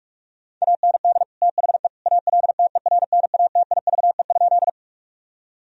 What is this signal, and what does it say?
Morse code, heard as a keyed tone, about 41 wpm, 700 Hz. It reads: AND THE ALTERNATIVE?
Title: Sentences from Top 1000 Words - Review Part 4 - 45wpm